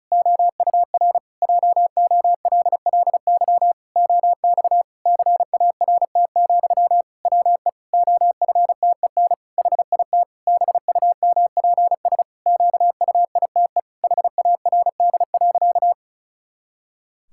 Morse code, heard as a keyed tone, about 34 wpm, 700 Hz. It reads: OUR JOLLY OX CART, WE OFTEN HIT BUMPS QUITE HARD.
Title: Pangram Challenge v2 - 35wpm